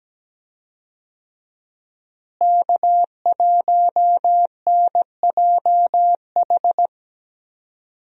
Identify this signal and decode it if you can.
Morse code, heard as a keyed tone, about 17 wpm, 700 Hz. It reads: K1NJH
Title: USA Call Signs - 5 Characters - Repeated 3x - 17wpm